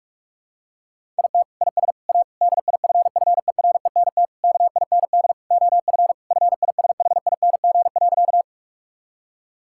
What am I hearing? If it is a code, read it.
Morse code, heard as a keyed tone, about 46 wpm, 700 Hz. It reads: IT IS A DIFFERENT KIND OF FISHING.
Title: Sentences from Top 1000 Words - Review Part 5 - 45wpm